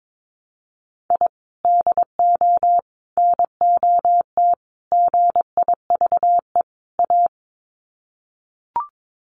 Morse code, heard as a keyed tone, about 22 wpm, 700 Hz. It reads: I DO NOT GIVE A
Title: Sets of 5 Words Encore2 22wpm